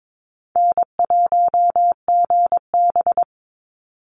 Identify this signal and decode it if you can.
Morse code, heard as a keyed tone, about 22 wpm, 700 Hz. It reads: N1GB
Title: NA SOTA Chaser Callsigns 22wpm